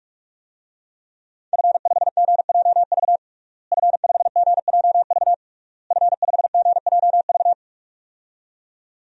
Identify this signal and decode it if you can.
Morse code, heard as a keyed tone, about 46 wpm, 700 Hz. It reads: F5GJV F5GJV F5GJV
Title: DXCC-100 Call Signs - 5 Characters - Repeated 3x - 45wpm